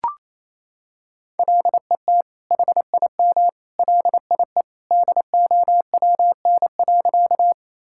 Morse code, heard as a keyed tone, about 28 wpm, 700 Hz. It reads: LET HIM LIE DOWN.
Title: Sentences from Top 1600 Words 28wpm